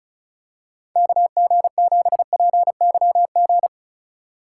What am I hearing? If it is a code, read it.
Morse code, heard as a keyed tone, about 34 wpm, 700 Hz. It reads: KG7PYG
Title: USA Call Signs - 6 Characters - 2x Character Spacing - 35wpm